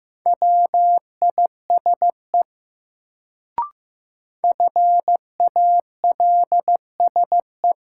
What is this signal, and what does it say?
Morse code, heard as a keyed tone, about 15 wpm, 700 Hz. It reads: WISE FALSE
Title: Sentences from Top 1600 Words - New Word Drill - 15wpm